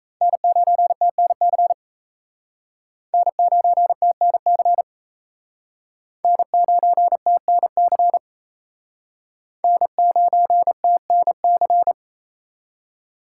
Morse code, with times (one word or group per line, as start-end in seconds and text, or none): 0.2-1.7 N9TNC
3.1-4.8 N9TNC
6.2-8.2 N9TNC
9.6-11.9 N9TNC